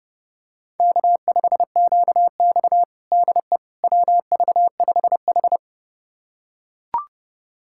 Morse code, heard as a keyed tone, about 30 wpm, 700 Hz. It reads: K5QX DE WV5H
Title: USA Callsign DE Callsign - 4 Characters - 30wpm